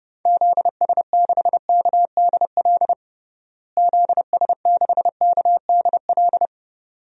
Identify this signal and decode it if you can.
Morse code, heard as a keyed tone, about 30 wpm, 700 Hz. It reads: ZS6KDL ZS6KDL
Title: DXCC-100 Call Signs - 6 Characters - Repeated 2x - 30wpm